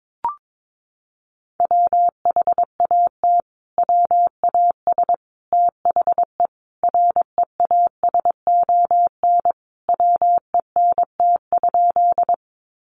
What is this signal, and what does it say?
Morse code, heard as a keyed tone, about 22 wpm, 700 Hz. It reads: WHAT WAS THE REASON WENT?
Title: Sentences from Top 700 Words 22wpm